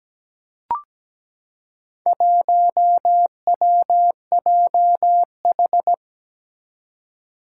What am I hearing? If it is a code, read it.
Morse code, heard as a keyed tone, about 17 wpm, 700 Hz. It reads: 1WJH
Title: Four Challenging Letter-Number Combinations 17wpm